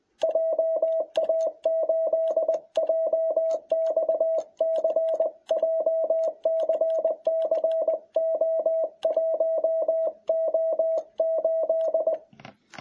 Morse code, heard as a keyed tone, about 20 wpm, 660 Hz. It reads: JA8JXC JCCO1O8